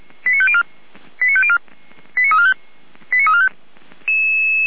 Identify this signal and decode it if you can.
Five-tone selective call (ZVEI), heard as quick runs of five tones, tones of about 70 ms. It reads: 89504 89504 89345 89345